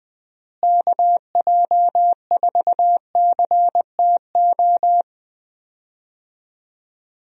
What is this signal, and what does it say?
Morse code, heard as a keyed tone, about 21 wpm, 700 Hz. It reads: KJ4CTO
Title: USA Call Signs - 6 Characters - Repeated 2x - 20wpm